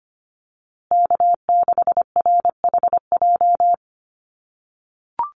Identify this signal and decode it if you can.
Morse code, heard as a keyed tone, about 25 wpm, 700 Hz. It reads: K6RHJ
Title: USA Call Signs Repeated 3x 25wpm